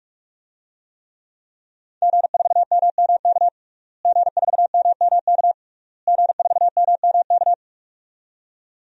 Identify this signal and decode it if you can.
Morse code, heard as a keyed tone, about 46 wpm, 700 Hz. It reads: G4MMK G4MMK G4MMK
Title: DXCC-100 Call Signs - 5 Characters - Repeated 3x - 45wpm